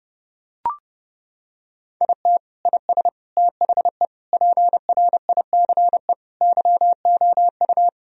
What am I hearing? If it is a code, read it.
Morse code, heard as a keyed tone, about 32 wpm, 700 Hz. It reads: IT IS THE PRICE YOU
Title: Sentences from Top 1000 Words - Review Part 4 - 30wpm